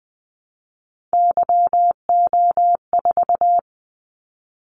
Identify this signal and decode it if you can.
Morse code, heard as a keyed tone, about 19 wpm, 700 Hz. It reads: YO4
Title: Callsign Prefixes for Top 25 DXCC - 20wpm